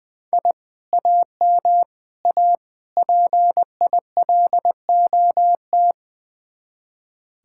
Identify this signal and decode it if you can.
Morse code, heard as a keyed tone, about 20 wpm, 700 Hz. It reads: I AM A PILOT
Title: Speed-Racing Common QSO Phrases Encore - 20wpm